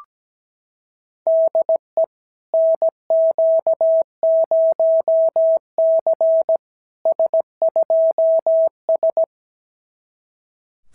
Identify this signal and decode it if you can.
Morse code, heard as a keyed tone, about 17 wpm, 650 Hz. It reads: DE NQ0C S2S
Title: SOTA QSO Practice 17wpm